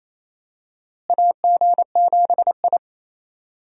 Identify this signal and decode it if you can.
Morse code, heard as a keyed tone, about 28 wpm, 700 Hz. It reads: AG7I
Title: USA Call Signs - 4 Characters - Repeated 3x - 28wpm